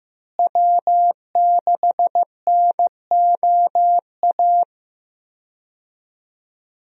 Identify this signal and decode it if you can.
Morse code, heard as a keyed tone, about 15 wpm, 700 Hz. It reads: W6NOA
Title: USA Call Signs - 5 Characters - Repeated 2x - 15wpm